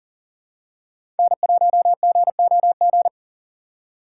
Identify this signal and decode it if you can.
Morse code, heard as a keyed tone, about 39 wpm, 700 Hz. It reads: N1GOG
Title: USA Call Signs - 5 Characters - 3x Character Spacing - 40wpm